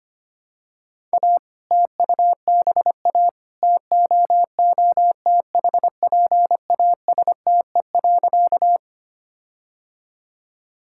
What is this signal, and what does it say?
Morse code, heard as a keyed tone, about 25 wpm, 700 Hz. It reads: A TUBA TOOTHPASTE.
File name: practice / 25wpm / All-Age Appropriate Jokes - 25wpm